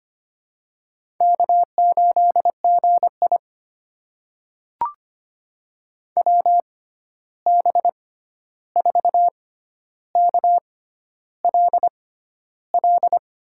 Morse code, heard as a keyed tone, about 25 wpm, spaced as for 11 wpm, 700 Hz. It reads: K8GI WB4KLL
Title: USA Call Signs - 6x Character Spacing - 25wpm